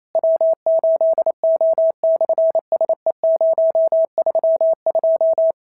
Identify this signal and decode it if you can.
Morse code, heard as a keyed tone, about 28 wpm, 650 Hz. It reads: W8O/SE032
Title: SOTA QSO Practice 28wpm